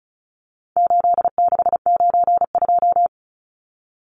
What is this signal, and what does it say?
Morse code, heard as a keyed tone, about 34 wpm, 700 Hz. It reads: 8692
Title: Numbers - 4-Digit 35wpm